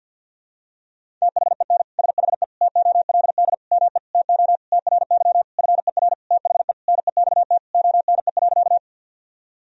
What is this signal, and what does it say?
Morse code, heard as a keyed tone, about 51 wpm, 700 Hz. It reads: THEN SHE TOLD ME TO TRY FER THE NEXT ONE.
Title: Sentences from Top 400 Words and Abbreviations 50wpm